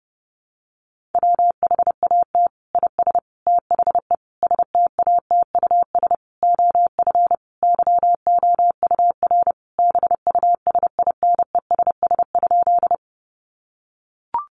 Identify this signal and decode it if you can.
Morse code, heard as a keyed tone, about 30 wpm, 700 Hz. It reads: WHAT IS THE STATUS OF YOUR BUSINESS?
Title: Sentences from Top 1400 Words 30wpm